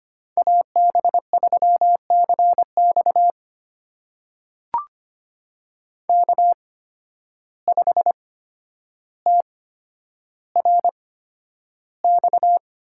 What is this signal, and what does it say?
Morse code, heard as a keyed tone, about 25 wpm, spaced as for 11 wpm, 700 Hz. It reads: AB3CX K5TRX
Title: USA Call Signs - 8x Character Spacing - 25wpm